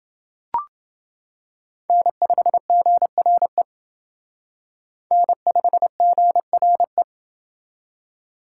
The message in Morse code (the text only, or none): N5GRE N5GRE